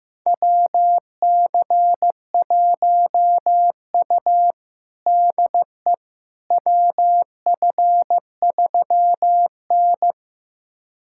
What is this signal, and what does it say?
Morse code, heard as a keyed tone, about 16 wpm, 700 Hz. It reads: WC1U DE WF3N
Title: USA Callsign DE Callsign - 4 Characters - 15wpm